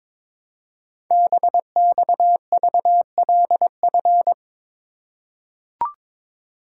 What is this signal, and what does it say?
Morse code, heard as a keyed tone, about 23 wpm, 700 Hz. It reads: BXVLF
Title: Code Groups - 5 Letters - 22wpm